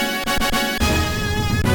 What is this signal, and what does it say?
There is no code.